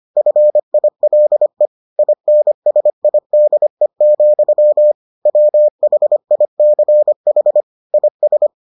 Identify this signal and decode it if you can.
Morse code, heard as a keyed tone, about 25 wpm, 600 Hz. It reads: FILE INSIDE, WHICH IS